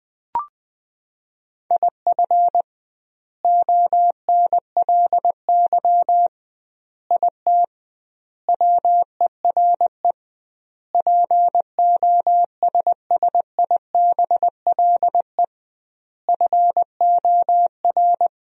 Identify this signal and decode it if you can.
Morse code, heard as a keyed tone, about 21 wpm, 700 Hz. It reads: IF ONLY IT WERE POSSIBLE FOR
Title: 2x Word-Spacing - Sentences from Top 500 Words - 20wpm